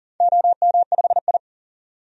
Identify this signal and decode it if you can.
Morse code, heard as a keyed tone, about 40 wpm, 700 Hz. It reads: OM5I
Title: DXCC-100 Call Signs - 4 Characters - 40wpm